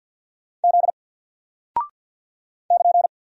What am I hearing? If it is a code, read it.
Morse code, heard as a keyed tone, about 48 wpm, 700 Hz. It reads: Z <KN>
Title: Single Character (Letter:Number:Punctuation:Prosign) 50wpm